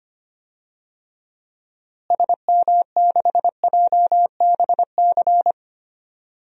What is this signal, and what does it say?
Morse code, heard as a keyed tone, about 25 wpm, 700 Hz. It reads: SM6JBC
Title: DXCC-100 Call Signs - 6 Characters - Repeated 2x - 25wpm